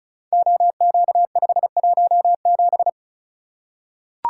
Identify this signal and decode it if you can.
Morse code, heard as a keyed tone, about 36 wpm, 700 Hz. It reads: OQ517
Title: Code Groups - 5 Letter:Numbers - 35wpm